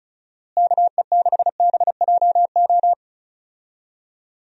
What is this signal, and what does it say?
Morse code, heard as a keyed tone, about 36 wpm, 700 Hz. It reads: KE6BJO
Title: USA Call Signs - 6 Characters - Repeated 3x - 35wpm